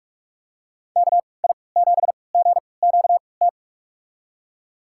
Morse code, heard as a keyed tone, about 46 wpm, spaced as for 26 wpm, 700 Hz. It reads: KI7GQT
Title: USA Call Signs - 6 Characters - 3x Character Spacing - 45wpm